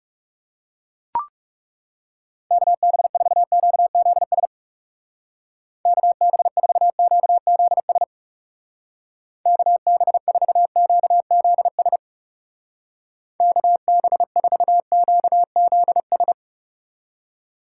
Morse code, heard as a keyed tone, about 36 wpm, 700 Hz. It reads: KB4QZS KB4QZS KB4QZS KB4QZS
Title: Speed Racing - USA 2x3 Call Signs 30wpm